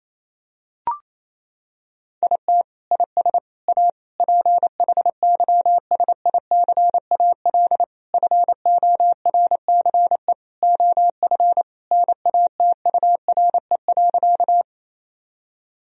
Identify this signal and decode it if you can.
Morse code, heard as a keyed tone, about 28 wpm, 700 Hz. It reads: IT IS A PHYSICAL FORCE OF NATURE.
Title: Sentences from Top 1000 Words - Review Part 4 - 28wpm